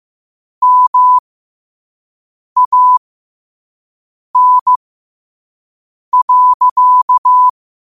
Morse code, heard as a keyed tone, about 15 wpm, spaced as for 5 wpm, 1000 Hz. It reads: MAN.
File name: qotdfiles / 5wpm